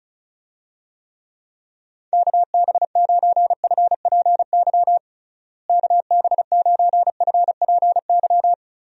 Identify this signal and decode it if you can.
Morse code, heard as a keyed tone, about 36 wpm, 700 Hz. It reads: KB9FPY KB9FPY
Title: USA Call Signs - 6 Characters - Repeated 3x - 35wpm